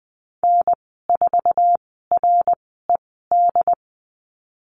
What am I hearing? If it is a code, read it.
Morse code, heard as a keyed tone, about 20 wpm, spaced as for 14 wpm, 700 Hz. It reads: N4RED
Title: 069 Lesson 12g - Letter d - Call Signs 20wpm